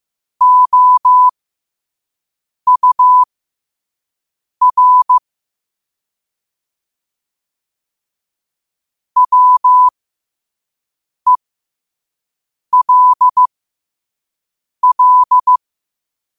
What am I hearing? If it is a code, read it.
Morse code, heard as a keyed tone, about 15 wpm, spaced as for 5 wpm, 1000 Hz. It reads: OUR WELL
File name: qotdfiles / 5wpm